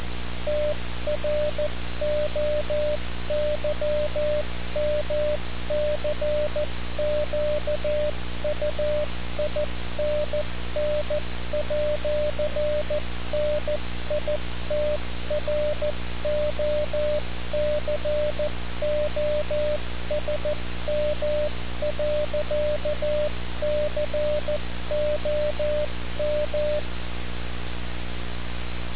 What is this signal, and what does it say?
Morse code, heard as a keyed tone, about 14 wpm, 600 Hz, with noise. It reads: TROYMCQUINN@NITROCOSM.COM
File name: radiocap / REC z